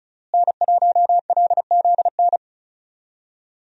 Morse code, heard as a keyed tone, about 34 wpm, 700 Hz. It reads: N1LZN
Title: USA Call Signs - 5 Characters - Repeated 3x - 35wpm